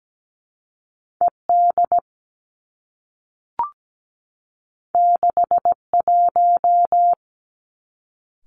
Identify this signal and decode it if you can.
Morse code, heard as a keyed tone, about 17 wpm, 700 Hz. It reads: E D 6 1
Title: Two Challenging Letter-Number Combinations 17wpm